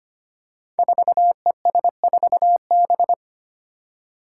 Morse code, heard as a keyed tone, about 25 wpm, 700 Hz. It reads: 4ES4B
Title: Five Challenging Letter-Number Combinations 25wpm